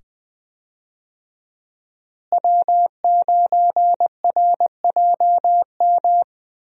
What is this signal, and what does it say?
Morse code, heard as a keyed tone, about 20 wpm, 700 Hz. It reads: W9RJM
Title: USA Call Signs - 5 Characters - Repeated 2x - 20wpm